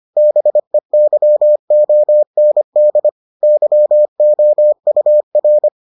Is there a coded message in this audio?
Yes: Morse code from 0.2 to 5.7 s, heard as a keyed tone, about 25 wpm, 600 Hz. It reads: BEYOND YOUR